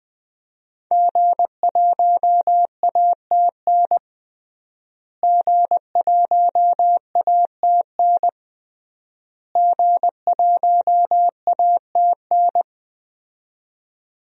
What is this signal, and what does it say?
Morse code, heard as a keyed tone, about 20 wpm, 700 Hz. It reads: G1ATN G1ATN G1ATN
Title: DXCC-100 Call Signs - 5 Characters - Repeated 3x - 20wpm